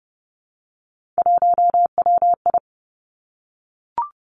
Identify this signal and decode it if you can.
Morse code, heard as a keyed tone, about 30 wpm, 700 Hz. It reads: 1WI
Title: Three Challenging Letter-Number Combinations 30wpm